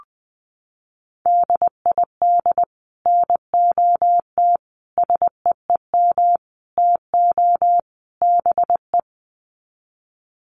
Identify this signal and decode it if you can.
Morse code, heard as a keyed tone, about 20 wpm, 700 Hz. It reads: DID NOT SEEM TO BE
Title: Sets of 5 Words Encore 20wpm